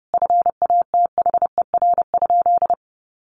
Morse code, heard as a keyed tone, about 30 wpm, 700 Hz. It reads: FATHER?
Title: Sentences from Top 1400 Words 30wpm